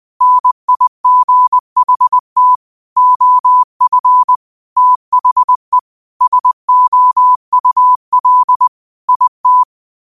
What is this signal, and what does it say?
Morse code, heard as a keyed tone, about 20 wpm, 1000 Hz. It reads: NIGHT OF THE SOUL IT